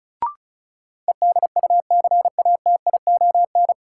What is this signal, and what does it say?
Morse code, heard as a keyed tone, about 34 wpm, 700 Hz. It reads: EDUCATION
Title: Sentences from Top 1100 Words - New Word Drill Alt - 35wpm